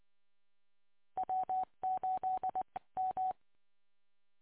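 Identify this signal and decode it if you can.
Morse code, heard as a keyed tone, about 21 wpm, 750 Hz. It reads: W8 M
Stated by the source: Audio archive of W8UM on 145.230.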